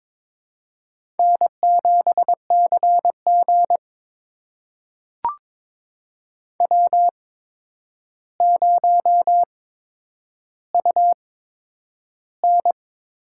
Morse code, heard as a keyed tone, about 22 wpm, spaced as for 9 wpm, 700 Hz. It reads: N7CG W0UN
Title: USA Call Signs - 4 Characters - 8x Character Spacing - 22wpm